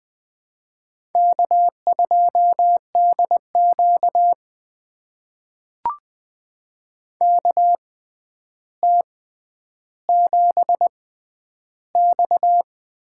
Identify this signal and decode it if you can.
Morse code, heard as a keyed tone, about 20 wpm, spaced as for 10 wpm, 700 Hz. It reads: K2DQ KT7X
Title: USA Call Signs - 4 Characters - 6x Character Spacing - 20wpm